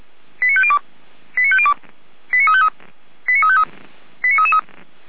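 Five-tone selective call (ZVEI), heard as quick runs of five tones, tones of about 70 ms. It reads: 89502 89502 89353 89353 89333
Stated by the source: ZVEI five-tone call 89350 received on SDR.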